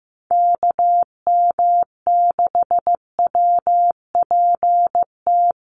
Morse code, heard as a keyed tone, about 15 wpm, 700 Hz. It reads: KM6WPT
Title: USA Call Signs Repeated 3x 15wpm